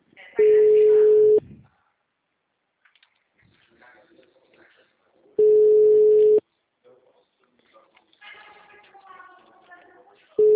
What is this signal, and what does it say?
Ringback tone (Europe, 425 Hz), with speech.